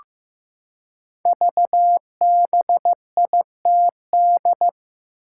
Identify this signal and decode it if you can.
Morse code, heard as a keyed tone, about 15 wpm, 700 Hz. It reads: VBITD
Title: Code Groups - 5 Letters - 15wpm